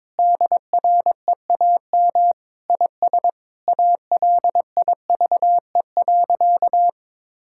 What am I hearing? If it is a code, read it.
Morse code, heard as a keyed tone, about 22 wpm, 700 Hz. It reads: DREAM IS ALIVE.
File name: practice / 22wpm / Sentences with 2-Word Phrases - 22wpm